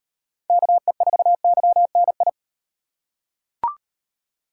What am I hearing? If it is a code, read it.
Morse code, heard as a keyed tone, about 37 wpm, 700 Hz. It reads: KE4YNI